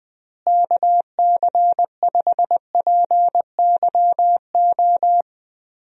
Morse code, heard as a keyed tone, about 20 wpm, 700 Hz. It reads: KC5PYO
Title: USA Call Signs - 6 Characters - Repeated 3x - 20wpm